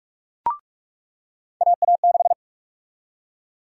Morse code, heard as a keyed tone, about 44 wpm, 700 Hz. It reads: AA6
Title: USA Call Sign Prefixes - 45wpm